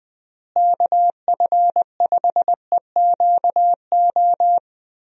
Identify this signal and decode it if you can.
Morse code, heard as a keyed tone, about 20 wpm, 700 Hz. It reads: KF5EQO
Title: USA Call Signs - 6 Characters - Repeated 2x - 20wpm